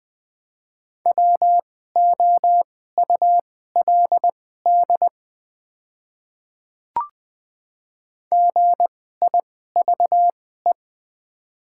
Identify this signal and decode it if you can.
Morse code, heard as a keyed tone, about 20 wpm, spaced as for 15 wpm, 700 Hz. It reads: WOULD GIVE
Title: 2x Character Spacing - Top 200 Words - 20wpm